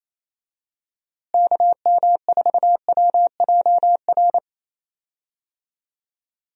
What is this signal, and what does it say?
Morse code, heard as a keyed tone, about 28 wpm, 700 Hz. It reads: KM4WJR
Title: USA Call Signs - 6 Characters - Repeated 2x - 28wpm